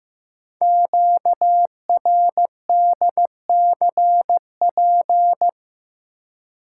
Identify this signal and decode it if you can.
Morse code, heard as a keyed tone, about 15 wpm, 700 Hz. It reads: QRDCP
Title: Code Groups - 5 Letter:Numbers - 15wpm